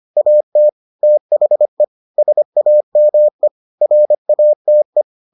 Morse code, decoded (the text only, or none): AT THE SAME RATE